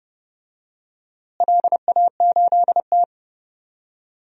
Morse code, heard as a keyed tone, about 31 wpm, 700 Hz. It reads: LA8T
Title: DXCC-100 Call Signs - 4 Characters - Repeated 3x - 30wpm